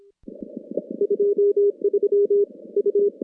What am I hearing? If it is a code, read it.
Morse code, heard as a keyed tone, about 26 wpm, 410 Hz. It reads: 23 U
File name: contests / CW-SS-k5tr- -2200